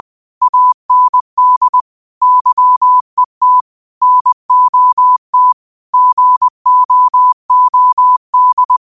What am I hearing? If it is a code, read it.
Morse code, heard as a keyed tone, about 20 wpm, 1000 Hz. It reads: AND YET NOT GOOD
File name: qotdfiles / 20wpm